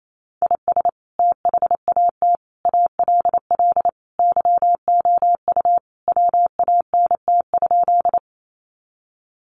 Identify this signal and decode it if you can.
Morse code, heard as a keyed tone, about 28 wpm, 700 Hz. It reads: IS THAT ALL YOU WANT?
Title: Sentences from Top 200 Words 28wpm